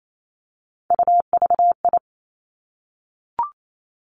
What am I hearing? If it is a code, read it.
Morse code, heard as a keyed tone, about 28 wpm, 700 Hz. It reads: UVI
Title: Three Challenging Letter-Number Combinations 28wpm